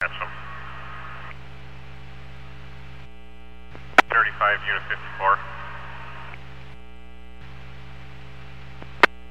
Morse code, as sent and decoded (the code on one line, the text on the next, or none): none